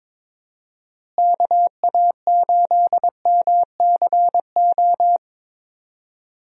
Morse code, heard as a keyed tone, about 22 wpm, 700 Hz. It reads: KA8MCO
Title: USA Call Signs - 6 Characters - Repeated 3x - 22wpm